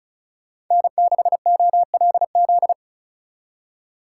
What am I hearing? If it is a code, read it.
Morse code, heard as a keyed tone, about 43 wpm, 700 Hz. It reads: N6OLZ
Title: Speed Racing - USA 1x3 Call Signs 35wpm